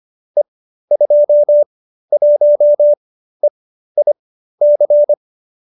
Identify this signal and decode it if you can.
Morse code, heard as a keyed tone, about 25 wpm, spaced as for 13 wpm, 570 Hz. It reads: E21EIC
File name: qso208b 13